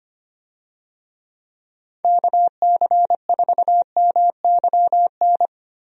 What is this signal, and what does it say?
Morse code, heard as a keyed tone, about 25 wpm, 700 Hz. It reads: KC4MYN
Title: USA Call Signs - 6 Characters - Repeated 2x - 25wpm